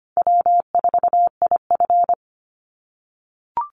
Morse code, heard as a keyed tone, about 25 wpm, 700 Hz. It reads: W4IF
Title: Call Signs - 25wpm